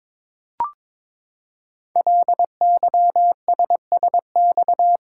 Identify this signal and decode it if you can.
Morse code, heard as a keyed tone, about 22 wpm, 700 Hz. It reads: LYSSX
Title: Code Groups - 5 Letters - 22wpm